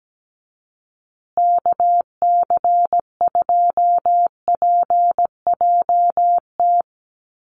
Morse code, heard as a keyed tone, about 17 wpm, 700 Hz. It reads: KC2PJT
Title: USA Call Signs Repeated 2x 17wpm